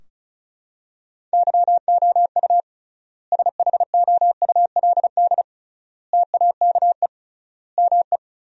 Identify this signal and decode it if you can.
Morse code, heard as a keyed tone, about 36 wpm, 700 Hz. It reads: YOU SHOULD TAKE ME
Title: 3x Word-Spacing - Sentences from Top 300 Words - 35wpm